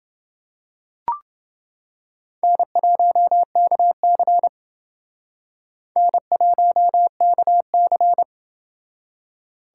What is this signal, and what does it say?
Morse code, heard as a keyed tone, about 29 wpm, 700 Hz. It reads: N1KC N1KC